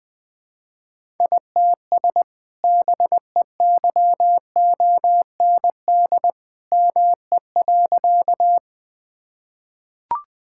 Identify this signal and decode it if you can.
Morse code, heard as a keyed tone, about 21 wpm, 700 Hz. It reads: ITS BEYOND ME.
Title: Sentences from Top 1000 Words - Review Part 6 - 20wpm